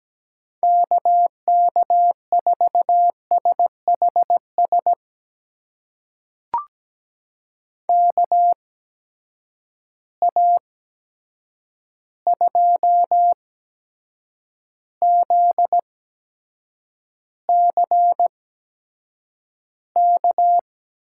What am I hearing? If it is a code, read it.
Morse code, heard as a keyed tone, about 17 wpm, spaced as for 7 wpm, 700 Hz. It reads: KK4SHS KA2ZCK
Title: USA Call Signs - 6 Characters - 8x Character Spacing - 17wpm